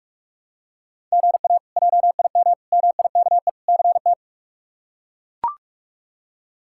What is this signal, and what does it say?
Morse code, heard as a keyed tone, about 42 wpm, 700 Hz. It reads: GA JIM MIKE CT
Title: K1USN SST QSO Practice - 45wpm